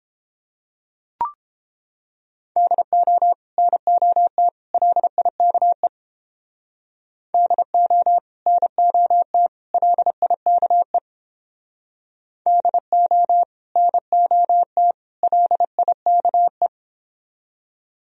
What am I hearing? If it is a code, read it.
Morse code, heard as a keyed tone, about 29 wpm, 700 Hz. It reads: DO NOT LIKE DO NOT LIKE DO NOT LIKE